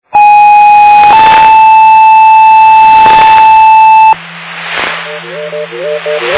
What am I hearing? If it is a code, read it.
Two-tone page, heard as a one-second tone, then a three-second tone, about 830 then 880 Hz, with noise. It is heard from 0.2 to 4.1 s.
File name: Marathon 154.3400 Fire EMS Paging